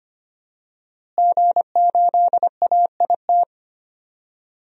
Morse code, heard as a keyed tone, about 25 wpm, 700 Hz. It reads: G8AIT
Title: DXCC-100 Call Signs - 5 Characters - Repeated 3x - 25wpm